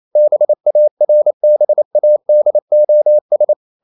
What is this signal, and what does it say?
Morse code, heard as a keyed tone, about 28 wpm, 600 Hz. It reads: BARBADOS
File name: CW / Practice / NationalCapitals28